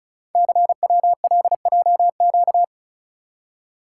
Morse code, heard as a keyed tone, about 34 wpm, 700 Hz. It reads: CWLJQ
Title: Code Groups - 5 Letters - 35wpm